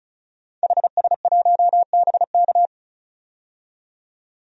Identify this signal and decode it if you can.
Morse code, heard as a keyed tone, about 34 wpm, 700 Hz. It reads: HS1BK
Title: DXCC-100 Call Signs - 5 Characters - 35wpm